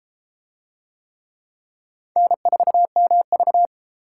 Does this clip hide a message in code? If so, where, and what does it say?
Yes: Morse code from 2.2 to 3.7 s, heard as a keyed tone, about 33 wpm, 700 Hz. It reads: N4MV